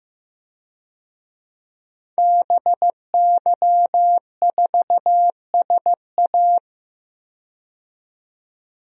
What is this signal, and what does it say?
Morse code, heard as a keyed tone, about 15 wpm, 700 Hz. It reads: BY4SA
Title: DXCC-100 Call Signs - 5 Characters - 15wpm